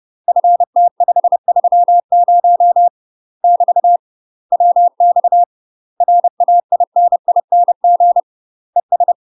Morse code, heard as a keyed tone, about 30 wpm, 700 Hz. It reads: FT530 <BT> WX RAINING ES
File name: UK MASTER QSOs TRAINING / 30WPM / UK MASTER QSOs 0036